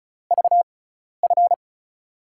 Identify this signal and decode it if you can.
Morse code, heard as a keyed tone, about 36 wpm, spaced as for 12 wpm, 700 Hz. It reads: VF
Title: USA Call Signs - 6 Characters - 6x Character Spacing - 35wpm